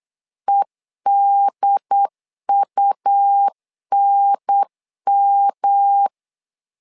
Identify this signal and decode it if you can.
Morse code, heard as a keyed tone, about 8 wpm, 800 Hz. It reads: EDUNM
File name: LEZIONI / 3 MWRO / 3 M 8wpm 2300